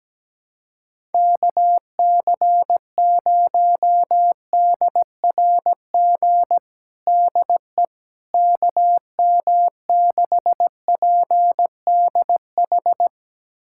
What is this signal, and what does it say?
Morse code, heard as a keyed tone, about 17 wpm, 700 Hz. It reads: KC0DRG DE KM6PDH
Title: USA Callsign DE Callsign - 6 Characters - 17wpm